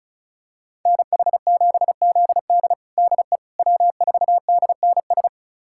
Morse code, heard as a keyed tone, about 36 wpm, 700 Hz. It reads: NH7ZD DE W4DNS
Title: USA Callsign DE Callsign - 5 Characters - 35wpm